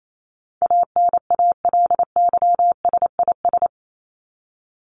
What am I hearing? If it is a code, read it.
Morse code, heard as a keyed tone, about 28 wpm, 700 Hz. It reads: ANALYSIS
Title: Top 501-750 Words 28wpm